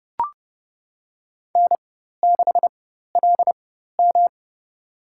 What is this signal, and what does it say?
Morse code, heard as a keyed tone, about 30 wpm, spaced as for 14 wpm, 700 Hz. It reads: N6LM
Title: USA Call Signs - 4x Character Spacing - 30wpm